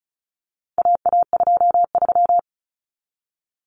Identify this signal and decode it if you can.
Morse code, heard as a keyed tone, about 34 wpm, 700 Hz. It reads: AA23
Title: Common Cut Numbers - 4-Digit 35wpm